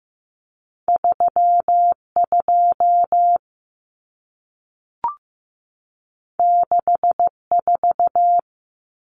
Morse code, heard as a keyed tone, about 15 wpm, 700 Hz. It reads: 32 64
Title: Numbers - 2-Digit 15wpm